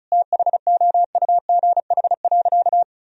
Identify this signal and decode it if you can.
Morse code, heard as a keyed tone, about 36 wpm, 700 Hz. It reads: THOUGH.
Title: Sentences from Top 1000 Words - Review Part 2 - 35wpm